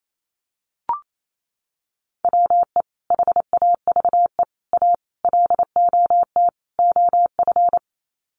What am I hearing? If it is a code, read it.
Morse code, heard as a keyed tone, about 28 wpm, 700 Hz. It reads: WE HAVE A LOT OF
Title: Sets of 5 Words 28wpm